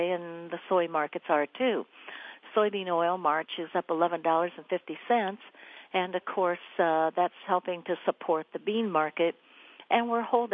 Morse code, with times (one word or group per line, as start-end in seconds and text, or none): none